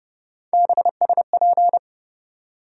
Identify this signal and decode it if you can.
Morse code, heard as a keyed tone, about 32 wpm, 700 Hz. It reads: BSP
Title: USA Call Sign Postfixes - 3 Letters 30wpm